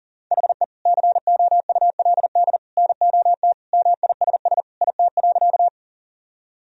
Morse code, heard as a keyed tone, about 40 wpm, 700 Hz. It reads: HE COULD NOT MISS IT.
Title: Sentences from Top 1000 Words - Review Part 3 - 40wpm